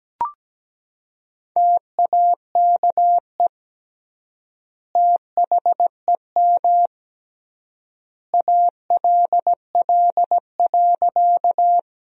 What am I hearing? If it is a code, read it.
Morse code, heard as a keyed tone, about 17 wpm, 700 Hz. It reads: TAKE THEM ALL.
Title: 3x Word-Spacing - Sentences from Top 200 Words - 17wpm